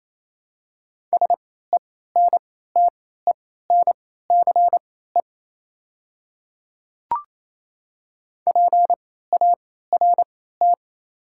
Morse code, heard as a keyed tone, about 28 wpm, spaced as for 16 wpm, 700 Hz. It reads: SENTENCE PART
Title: 3x Character Spacing - Top 200 Words - 28wpm